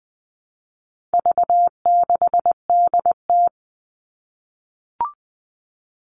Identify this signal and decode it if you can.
Morse code, heard as a keyed tone, about 20 wpm, 700 Hz. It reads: V6DT
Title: All Cut Numbers - 4-Digit 20wpm